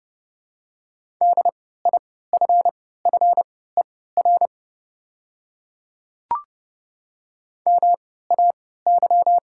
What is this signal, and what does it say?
Morse code, heard as a keyed tone, about 32 wpm, spaced as for 17 wpm, 700 Hz. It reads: DIFFER MAY
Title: 3x Character Spacing - Top 300 Words - 30wpm